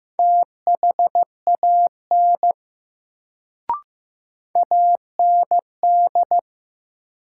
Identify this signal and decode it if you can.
Morse code, heard as a keyed tone, about 15 wpm, 700 Hz. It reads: THAN AND
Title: Sentences from Top 100 Words - New Word Drill - 15wpm